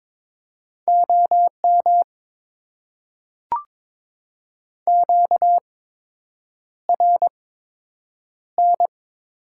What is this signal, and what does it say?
Morse code, heard as a keyed tone, about 22 wpm, spaced as for 8 wpm, 700 Hz. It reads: OM QRN
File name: Top 90 QSO Elements - 8x Character Spacing - 22wpm